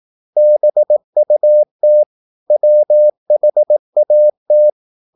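Morse code, heard as a keyed tone, about 18 wpm, 600 Hz. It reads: BUT WHAT
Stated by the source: Morse practice at 18 wpm, English text, from CW med Gnister.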